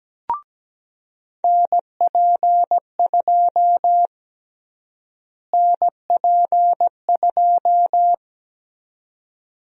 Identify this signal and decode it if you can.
Morse code, heard as a keyed tone, about 17 wpm, 700 Hz. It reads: NP2 NP2
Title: USA Call Sign Prefixes - Repeated 2x - 17wpm